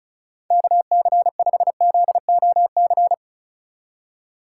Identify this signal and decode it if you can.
Morse code, heard as a keyed tone, about 43 wpm, 700 Hz. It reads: KC5ZOC
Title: Speed Racing - USA Call Signs 35wpm